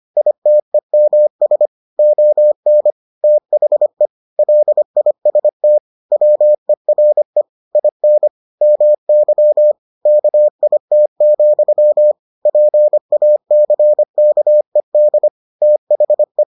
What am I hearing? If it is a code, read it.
Morse code, heard as a keyed tone, about 25 wpm, 600 Hz. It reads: ITEMS ON THE LIST WERE IN MY KIT, PACKED THE